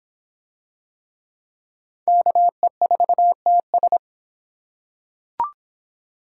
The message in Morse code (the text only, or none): KE4TS